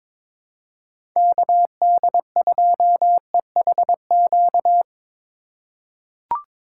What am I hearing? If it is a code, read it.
Morse code, heard as a keyed tone, about 22 wpm, 700 Hz. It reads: KD2EHQ
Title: USA Call Signs - 6 Characters - 3x Character Spacing - 22wpm